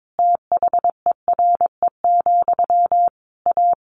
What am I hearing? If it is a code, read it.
Morse code, heard as a keyed tone, about 22 wpm, 700 Hz. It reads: THERE, A
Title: A Sound Of Thunder 22wpm